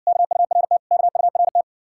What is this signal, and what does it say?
Morse code, heard as a keyed tone, about 57 wpm, 700 Hz. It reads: KURT KURT